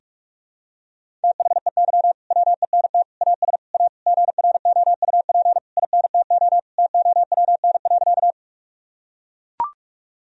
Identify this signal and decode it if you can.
Morse code, heard as a keyed tone, about 42 wpm, 700 Hz. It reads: THEY WENT AS A GROUP INTO TOWN.
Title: Sentences from Top 1000 Words - Review Part 1 - 45wpm